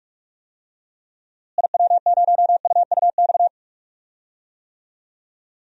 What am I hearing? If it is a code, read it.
Morse code, heard as a keyed tone, about 47 wpm, 700 Hz. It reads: IW0UUX
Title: DXCC-100 Call Signs - 6 Characters - Repeated 2x - 45wpm